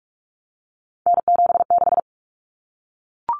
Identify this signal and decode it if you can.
Morse code, heard as a keyed tone, about 44 wpm, 700 Hz. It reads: N76
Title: Common Cut Numbers - 3-Digit 45wpm